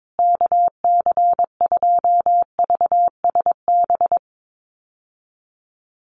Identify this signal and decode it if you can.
Morse code, heard as a keyed tone, about 22 wpm, 700 Hz. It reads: KC2VSB
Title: USA Call Signs Repeated 3x 22wpm